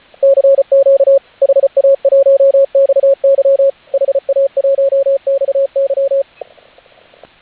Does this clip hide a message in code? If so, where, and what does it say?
Yes: Morse code from 0.2 to 6.2 s, heard as a keyed tone, about 34 wpm, 560 Hz, with radio noise. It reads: CQ HA1XY HA1XY